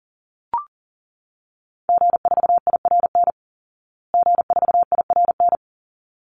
Morse code, heard as a keyed tone, about 40 wpm, 700 Hz. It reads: G4IRN G4IRN
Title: Call Signs Repeated 3x - 40wpm